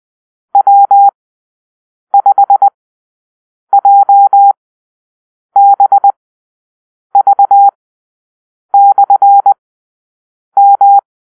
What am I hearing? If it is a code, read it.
Morse code, heard as a keyed tone, about 20 wpm, spaced as for 7 wpm, 800 Hz. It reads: W5JBV/M
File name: arrl / Call-Sign Drill-8